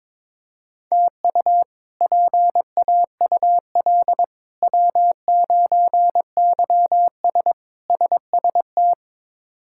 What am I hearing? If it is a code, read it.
Morse code, heard as a keyed tone, about 21 wpm, 700 Hz. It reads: TU PAUL W9YS SST
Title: K1USN SST QSO Practice - 22wpm